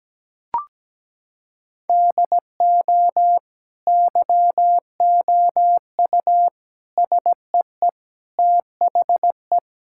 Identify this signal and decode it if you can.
Morse code, heard as a keyed tone, about 17 wpm, 700 Hz. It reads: DO YOU SEE THE
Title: Sentences from Top 1600 Words 17wpm